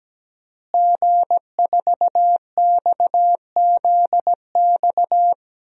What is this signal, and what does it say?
Morse code, heard as a keyed tone, about 17 wpm, 700 Hz. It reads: G4XZX
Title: DXCC-100 Call Signs - 5 Characters - Repeated 3x - 17wpm